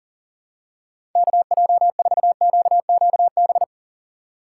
Morse code, heard as a keyed tone, about 40 wpm, 700 Hz. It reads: KJ4QQB
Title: USA Call Signs - 6 Characters - 4x Character Spacing - 40wpm